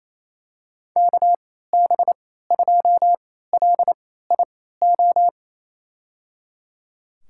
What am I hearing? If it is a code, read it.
Morse code, heard as a keyed tone, about 27 wpm, spaced as for 16 wpm, 700 Hz. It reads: KB2LIO
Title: USA Call Signs - 3x Character Spacing - 28wpm